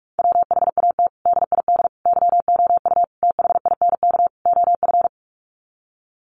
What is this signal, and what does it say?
Morse code, heard as a keyed tone, about 44 wpm, 700 Hz. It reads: WHAT DID YOU THINK OF
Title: Sets of 5 Words Encore2 45wpm